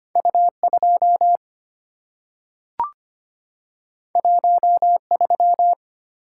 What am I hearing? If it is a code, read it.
Morse code, heard as a keyed tone, about 25 wpm, 700 Hz. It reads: U2 13
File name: Two Letter-Number Combinations - 25wpm